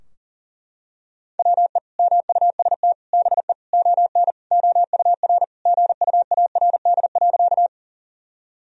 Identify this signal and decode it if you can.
Morse code, heard as a keyed tone, about 40 wpm, 700 Hz. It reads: WE MUST BE ON OUR GUARD.
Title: Sentences from Top 1000 Words - Review Part 4 - 40wpm